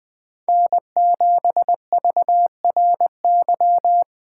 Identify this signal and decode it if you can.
Morse code, heard as a keyed tone, about 24 wpm, 700 Hz. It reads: N7VRY
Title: Speed Racing - USA 1x3 Call Signs 20wpm